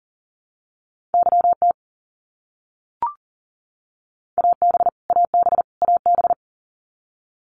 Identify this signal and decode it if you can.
Morse code, heard as a keyed tone, about 40 wpm, 700 Hz. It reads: YT AB AB AB
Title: Canadian Provinces and Territories - Repeated 3x - 40wpm